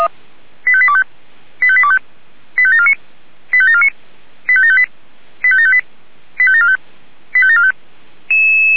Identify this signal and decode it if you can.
Five-tone selective call (ZVEI), heard as quick runs of five tones, tones of about 70 ms. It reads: 85825 85825 85749 85749 85758 85758 85745 85745
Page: ZVEI